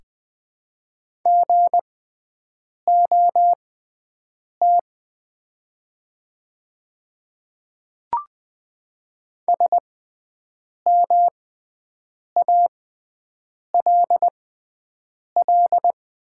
Morse code, heard as a keyed tone, about 20 wpm, spaced as for 7 wpm, 700 Hz. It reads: GOT SMALL
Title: 6x Character Spacing - Top 500 Words - 20wpm